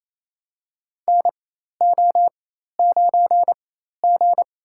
Morse code, heard as a keyed tone, about 28 wpm, spaced as for 13 wpm, 700 Hz. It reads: NO9G
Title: USA Call Signs - 4x Character Spacing - 28wpm